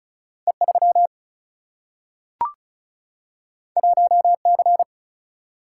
Morse code, heard as a keyed tone, about 36 wpm, 700 Hz. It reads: E3 1C
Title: Single-Letter Single-Number 35wpm